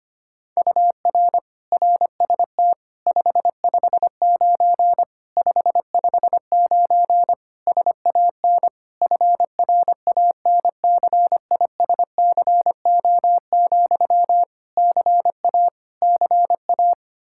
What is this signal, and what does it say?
Morse code, heard as a keyed tone, about 25 wpm, 700 Hz. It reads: UR RST 559 559 SAN FRANCISCO, CA CA
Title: Practice QSO - Exchange 1 - Beginning - 25wpm